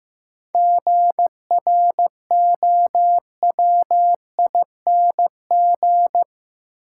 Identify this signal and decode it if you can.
Morse code, heard as a keyed tone, about 15 wpm, 700 Hz. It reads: GROWING
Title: Sets of 5 Words Encore - New Word Drill - 15wpm